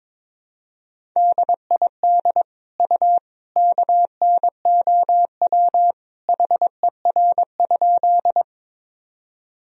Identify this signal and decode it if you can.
Morse code, heard as a keyed tone, about 21 wpm, 700 Hz. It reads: DID U KNOW HER?
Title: Sentences from Top 100 Words and Abbreviations 22wpm